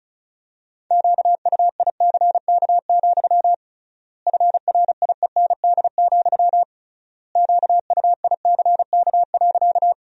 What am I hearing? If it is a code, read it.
Morse code, heard as a keyed tone, about 36 wpm, 700 Hz. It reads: QUICK, FRIEND, QUICK.
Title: 3x Word-Spacing - Sentences from Top 500 Words - 35wpm